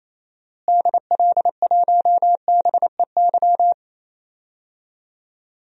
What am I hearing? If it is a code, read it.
Morse code, heard as a keyed tone, about 28 wpm, 700 Hz. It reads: DL1BEY
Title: DXCC-100 Call Signs - 6 Characters - Repeated 2x - 28wpm